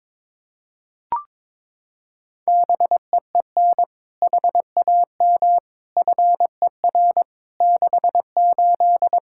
Morse code, heard as a keyed tone, about 22 wpm, 700 Hz. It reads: BEEN HAM FER 68
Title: K1USN SST QSO Practice 22wpm